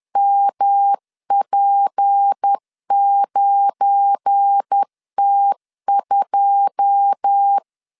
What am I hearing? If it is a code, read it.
Morse code, heard as a keyed tone, about 10 wpm, 800 Hz. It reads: MP9T2
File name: LEZIONI / 9 2 10wpm 2736